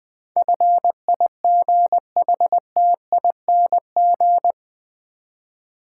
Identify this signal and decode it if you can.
Morse code, heard as a keyed tone, about 21 wpm, 700 Hz. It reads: FIGHTING
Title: Eight Letter Words - 20wpm